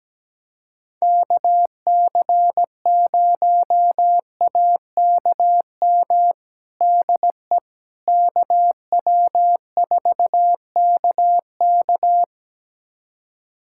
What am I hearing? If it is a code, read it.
Morse code, heard as a keyed tone, about 17 wpm, 700 Hz. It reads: KC0AKM DE KW4KK
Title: USA Callsign DE Callsign 17wpm